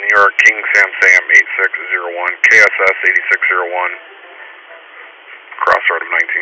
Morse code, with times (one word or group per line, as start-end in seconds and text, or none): none